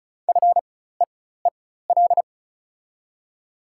Morse code, heard as a keyed tone, about 36 wpm, spaced as for 16 wpm, 700 Hz. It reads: FEEL
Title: 4x Character Spacing - Top 500 Words - 35wpm